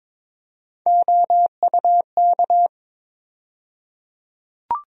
Morse code, heard as a keyed tone, about 22 wpm, 700 Hz. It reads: OUK
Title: USA Call Sign Postfixes - 3 Letters 22wpm